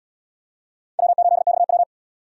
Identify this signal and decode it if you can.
Morse code, heard as a keyed tone, about 77 wpm, 700 Hz. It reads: W8XW